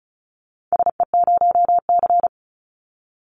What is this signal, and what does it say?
Morse code, heard as a keyed tone, about 36 wpm, 700 Hz. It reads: SE0C
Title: Call Signs - 35wpm